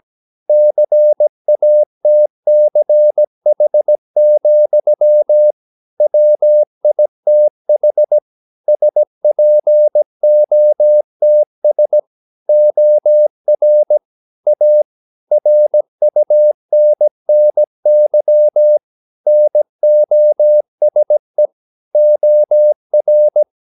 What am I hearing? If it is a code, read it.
Morse code, heard as a keyed tone, about 17 wpm, 600 Hz. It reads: CATCH, WITH SPOTS OR A RUNNY NOSE OR
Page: Romanen Never af Ken Follet på engelsk.